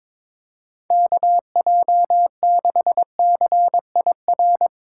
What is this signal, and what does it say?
Morse code, heard as a keyed tone, about 22 wpm, 700 Hz. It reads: KJ6CIR
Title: Speed Racing - USA 2x3 Call Signs 22wpm